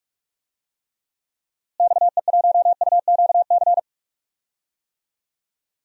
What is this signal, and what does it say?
Morse code, heard as a keyed tone, about 43 wpm, 700 Hz. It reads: XE1UQC
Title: DXCC-100 Call Signs - 6 Characters - Repeated 3x - 45wpm